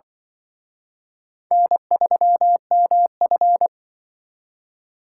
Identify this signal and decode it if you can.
Morse code, heard as a keyed tone, about 23 wpm, 700 Hz. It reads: N3MF
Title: Speed Racing - USA 1x2 Call Signs 20wpm